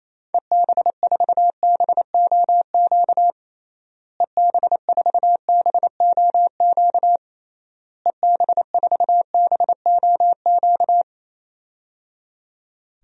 Morse code, heard as a keyed tone, about 27 wpm, 700 Hz. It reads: EB4BOQ EB4BOQ EB4BOQ
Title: DXCC-100 Call Signs - 6 Characters - Repeated 3x - 28wpm